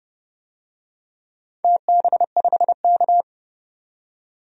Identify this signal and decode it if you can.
Morse code, heard as a keyed tone, about 30 wpm, 700 Hz. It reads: TB5K
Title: DXCC-100 Call Signs - 4 Characters - 30wpm